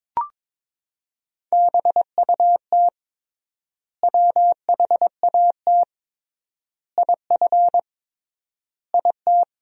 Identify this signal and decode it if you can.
Morse code, heard as a keyed tone, about 22 wpm, 700 Hz. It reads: BUT WHAT IF IT
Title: 3x Word-Spacing - Sentences from Top 100 Words - 22wpm